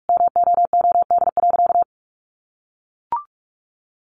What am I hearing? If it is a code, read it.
Morse code, heard as a keyed tone, about 44 wpm, 700 Hz. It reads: MOOD.
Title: Sentences from Top 1100 Words 45wpm